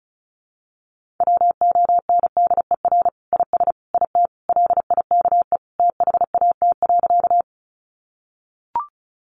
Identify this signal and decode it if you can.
Morse code, heard as a keyed tone, about 34 wpm, 700 Hz. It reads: WONDER IS IT LIKE THAT.
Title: Sentences from Top 500 Words 35wpm